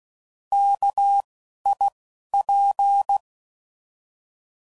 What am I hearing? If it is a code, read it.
Morse code, heard as a keyed tone, about 16 wpm, 800 Hz. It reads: K I P
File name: Prefixes / By First Letter / Ks